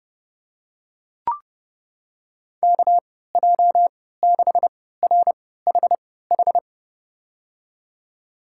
Morse code, heard as a keyed tone, about 30 wpm, spaced as for 17 wpm, 700 Hz. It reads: KJ6RHH
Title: USA Call Signs - 6 Characters - 3x Character Spacing - 30wpm